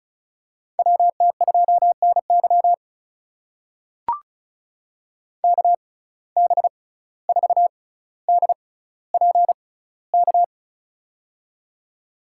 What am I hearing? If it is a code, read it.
Morse code, heard as a keyed tone, about 36 wpm, spaced as for 17 wpm, 700 Hz. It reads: WT2NY KB4DPK
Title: USA Call Signs - 6x Character Spacing - 35wpm